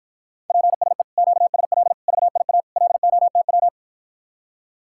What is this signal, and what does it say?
Morse code, heard as a keyed tone, about 54 wpm, 700 Hz. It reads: PSE QSL VIA LOTW